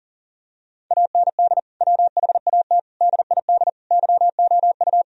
Morse code, heard as a keyed tone, about 40 wpm, 700 Hz. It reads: AND WHAT DID YOU
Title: Sentences from Top 1000 Words - Review Part 4 - 40wpm